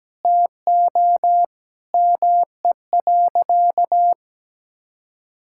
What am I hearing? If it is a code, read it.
Morse code, heard as a keyed tone, about 17 wpm, 700 Hz. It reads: TO ME.
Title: Sentences from Top 1600 Words 17wpm